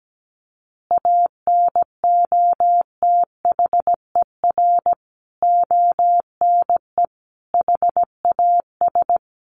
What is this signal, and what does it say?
Morse code, heard as a keyed tone, about 17 wpm, 700 Hz. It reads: ANOTHER ONE HAS
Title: Additional Easy Phrases 17wpm